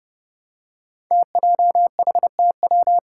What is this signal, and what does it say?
Morse code, heard as a keyed tone, about 30 wpm, 700 Hz. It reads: TJHTW
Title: Five Challenging Letter-Number Combinations 30wpm